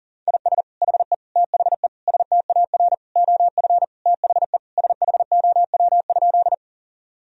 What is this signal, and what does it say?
Morse code, heard as a keyed tone, about 37 wpm, 700 Hz. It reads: IS HE THE STAR OF THE SHOW?
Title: Sentences from Top 1000 Words - Review Part 2 - 40wpm